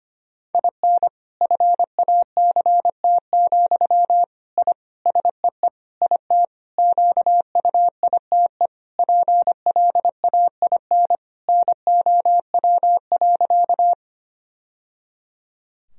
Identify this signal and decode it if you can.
Morse code, heard as a keyed tone, about 25 wpm, 700 Hz. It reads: IN FACT, I SEE IT QUITE PLAIN NOW.
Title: Sentences from Top 1000 Words - Review Part 3 - 25wpm